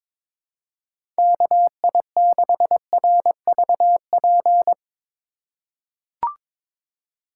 Morse code, heard as a keyed tone, about 21 wpm, 700 Hz. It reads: KI6RVP
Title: USA Call Signs - 6 Characters - 6x Character Spacing - 22wpm